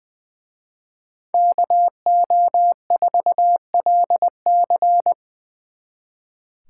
Morse code, heard as a keyed tone, about 22 wpm, 700 Hz. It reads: KO4LC
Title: Speed Racing - USA 2x2 Call Signs 20wpm